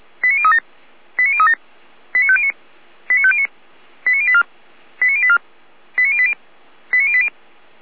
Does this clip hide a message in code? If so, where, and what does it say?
Yes: five-tone selective call (ZVEI) from 0.2 to 7.3 s, heard as quick runs of five tones, tones of about 70 ms. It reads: 89037 89037 89509 89509 89084 89084 89080 89080